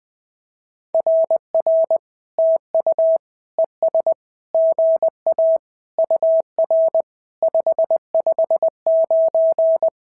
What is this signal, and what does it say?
Morse code, heard as a keyed tone, about 20 wpm, 650 Hz. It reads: RR TU ES GA UR 559
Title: SOTA QSO Practice 20wpm